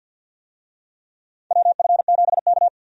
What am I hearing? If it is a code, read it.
Morse code, heard as a keyed tone, about 46 wpm, 700 Hz. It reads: WF7K
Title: USA Call Signs - 4 Characters - 50wpm